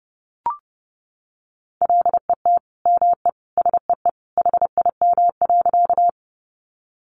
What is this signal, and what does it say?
Morse code, heard as a keyed tone, about 30 wpm, 700 Hz. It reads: LET ME SEE HIM.
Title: Sentences from Top 300 Words 30wpm